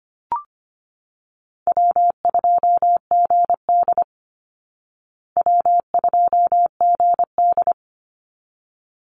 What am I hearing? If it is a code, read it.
Morse code, heard as a keyed tone, about 25 wpm, 700 Hz. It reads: W2GD W2GD
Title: Easy Call Signs Repeated 2x - 25wpm